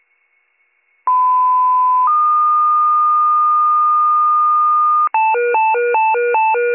Two-tone page, heard as a one-second tone, then a three-second tone, about 1010 then 1220 Hz. It is heard from 1.1 to 5.1 s.